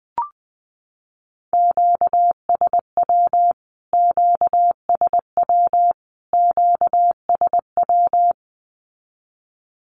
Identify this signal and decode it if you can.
Morse code, heard as a keyed tone, about 20 wpm, 700 Hz. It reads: QSW QSW QSW
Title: Q-Codes Repeated 3x 20wpm